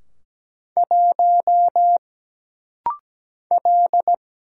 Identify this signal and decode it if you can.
Morse code, heard as a keyed tone, about 17 wpm, 700 Hz. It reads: 1 L
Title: Single Letter-Number - Rapid-Fire 17wpm